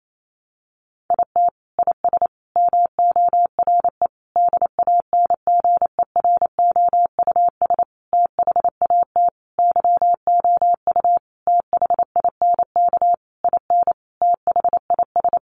Morse code, heard as a keyed tone, about 28 wpm, 700 Hz. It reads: IT IS MORE DANGEROUS THAT YOU THINK IN THIS
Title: Sentences from Top 1200 Words 28wpm